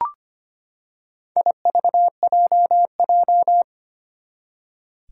Morse code, heard as a keyed tone, about 25 wpm, 700 Hz. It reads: IVJJ
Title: Four Challenging Letter-Number Combinations 25wpm